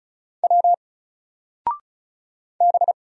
Morse code, heard as a keyed tone, about 36 wpm, 700 Hz. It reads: W B
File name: Single-Letter-Number-35wpm